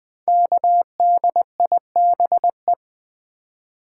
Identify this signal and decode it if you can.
Morse code, heard as a keyed tone, about 20 wpm, 700 Hz. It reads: KDIBE
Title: Code Groups - 5 Letters - 20wpm